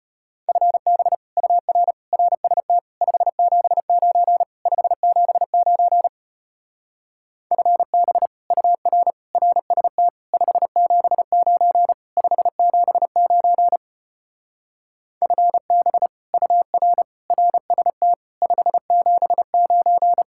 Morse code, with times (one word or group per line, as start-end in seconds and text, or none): 0.5-1.1 FB
1.4-1.9 UR
2.1-2.8 RST
3.0-4.4 579
4.7-6.1 579
7.5-8.3 FB
8.5-9.1 UR
9.3-10.1 RST
10.3-11.9 579
12.2-13.8 579
15.2-16.1 FB
16.3-17.0 UR
17.3-18.1 RST
18.4-20.2 579